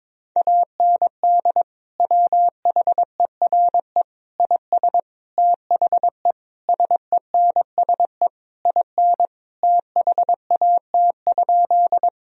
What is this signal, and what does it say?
Morse code, heard as a keyed tone, about 22 wpm, 700 Hz. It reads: AND WHERE IS THE SENSE IN THAT?
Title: Sentences from Top 1000 Words - Review Part 4 - 22wpm